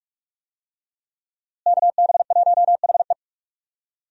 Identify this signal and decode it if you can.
Morse code, heard as a keyed tone, about 46 wpm, 700 Hz. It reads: KB1HE
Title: USA Call Signs - 5 Characters - Repeated 3x - 45wpm